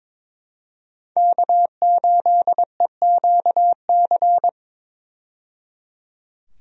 Morse code, heard as a keyed tone, about 22 wpm, 700 Hz. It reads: K8EQC
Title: USA Call Signs - 5 Characters - 22wpm